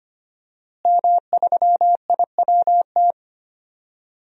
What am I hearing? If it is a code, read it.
Morse code, heard as a keyed tone, about 25 wpm, 700 Hz. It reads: M3IWT
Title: DXCC-100 Call Signs - 5 Characters - Repeated 3x - 25wpm